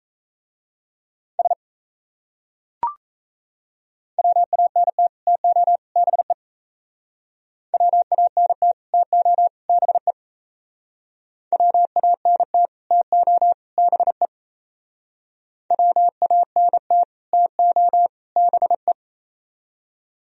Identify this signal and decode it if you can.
Morse code, heard as a keyed tone, about 37 wpm, 700 Hz. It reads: S WANT TO BE WANT TO BE WANT TO BE WANT TO BE